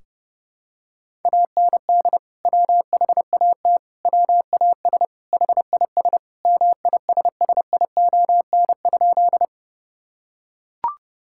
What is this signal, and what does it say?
Morse code, heard as a keyed tone, about 30 wpm, 700 Hz. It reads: AND WHAT WAS HIS MISSION?
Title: Sentences from Top 1600 Words 30wpm